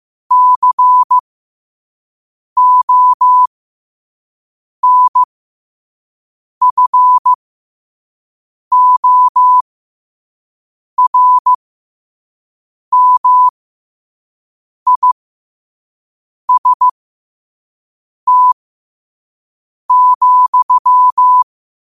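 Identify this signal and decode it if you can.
Morse code, heard as a keyed tone, about 15 wpm, spaced as for 5 wpm, 1000 Hz. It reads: CONFORMIST,
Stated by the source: Morse code sent at 5 wpm.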